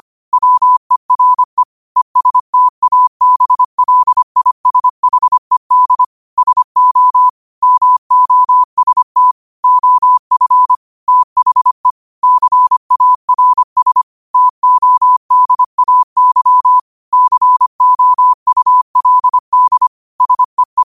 Morse code, heard as a keyed tone, about 25 wpm, 1000 Hz. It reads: WERE ESTABLISHED SO MOST OF THE CARS TODAY COULD SEE
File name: qotdfiles / 25wpm